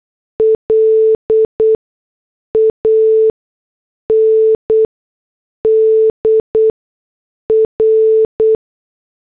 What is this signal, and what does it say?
Morse code, heard as a keyed tone, about 8 wpm, spaced as for 6 wpm, 440 Hz. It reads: LAND R